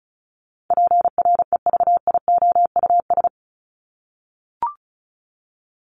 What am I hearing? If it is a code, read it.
Morse code, heard as a keyed tone, about 34 wpm, 700 Hz. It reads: PREVIOUS
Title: Top 1251-1500 Words 35wpm